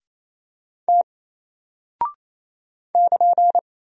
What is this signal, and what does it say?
Morse code, heard as a keyed tone, about 27 wpm, 700 Hz. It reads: T <KN>
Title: Single Character (Letter:Number:Punctuation:Prosign) 28wpm